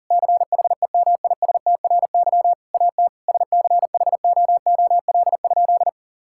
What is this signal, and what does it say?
Morse code, heard as a keyed tone, about 40 wpm, 700 Hz. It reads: CHEMISTRY AT SCHOOL?
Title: Sentences from Top 1000 Words - Review Part 6 - 40wpm